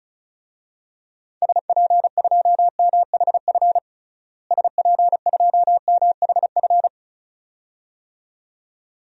Morse code, heard as a keyed tone, about 36 wpm, 700 Hz. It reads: SP2MHF SP2MHF
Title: DXCC-100 Call Signs - 6 Characters - Repeated 2x - 35wpm